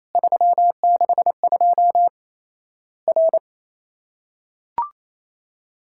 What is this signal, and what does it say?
Morse code, heard as a keyed tone, about 28 wpm, 700 Hz. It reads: 362 R
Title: Worked All Europe DX Contest - QTC Exchange - 28wpm